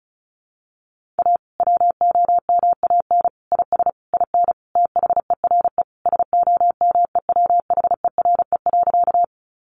Morse code, heard as a keyed tone, about 34 wpm, 700 Hz. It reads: A WOMAN IS IN THERE SOMEWHERE.
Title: Sentences from Top 1400 Words 35wpm